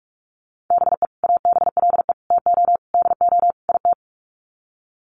Morse code, heard as a keyed tone, about 44 wpm, 700 Hz. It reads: BE ABLE TO DO IT
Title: Sets of 5 Words Encore2 45wpm